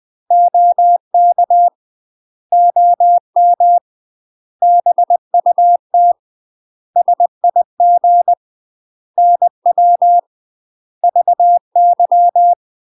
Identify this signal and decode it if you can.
Morse code, heard as a keyed tone, about 20 wpm, 700 Hz. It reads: OK OM BUT SIG NW VY
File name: UK MASTER QSOs TRAINING / 20WPM / UK MASTER QSOs 0003